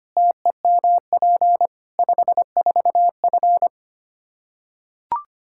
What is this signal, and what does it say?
Morse code, heard as a keyed tone, about 25 wpm, 700 Hz. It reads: TEMP 54F
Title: Practice QSO - Exchange 2 - Info - 25wpm